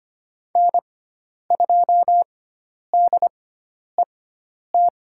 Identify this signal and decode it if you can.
Morse code, heard as a keyed tone, about 25 wpm, spaced as for 10 wpm, 700 Hz. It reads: N2DET
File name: Lesson 16g - Letter c - Call Signs 25wpm